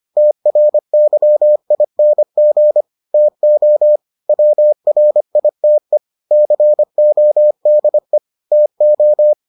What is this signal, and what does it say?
Morse code, heard as a keyed tone, about 25 wpm, 600 Hz. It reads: TRYING TO WRITE CODE TO